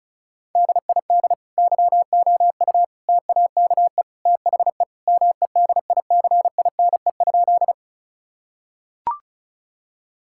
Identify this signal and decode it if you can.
Morse code, heard as a keyed tone, about 36 wpm, 700 Hz. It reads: DID YOU TAKE THE MEDICINE?
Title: Sentences from Top 1000 Words - Review Part 6 - 35wpm